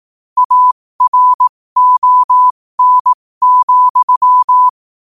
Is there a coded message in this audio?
Yes: Morse code from 0.4 to 4.7 s, heard as a keyed tone, about 18 wpm, spaced as for 15 wpm, 1000 Hz. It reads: ARON,